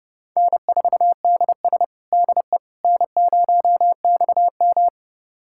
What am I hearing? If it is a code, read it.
Morse code, heard as a keyed tone, about 30 wpm, 700 Hz. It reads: N4DS DE N0XM
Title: USA Callsign DE Callsign - 4 Characters - 30wpm